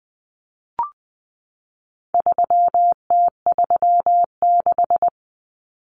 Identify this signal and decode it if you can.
Morse code, heard as a keyed tone, about 20 wpm, 700 Hz. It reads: 3T36
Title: Common Cut Numbers - 4-Digit 20wpm